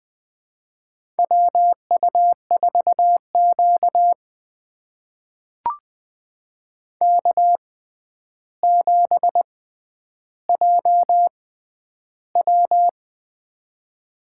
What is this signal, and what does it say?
Morse code, heard as a keyed tone, about 20 wpm, spaced as for 10 wpm, 700 Hz. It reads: WU4Q K7JW
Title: USA Call Signs - 4 Characters - 6x Character Spacing - 20wpm